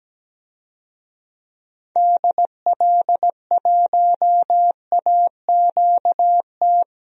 Morse code, heard as a keyed tone, about 17 wpm, 700 Hz. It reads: DL1AQT
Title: DXCC-100 Call Signs - 6 Characters - 17wpm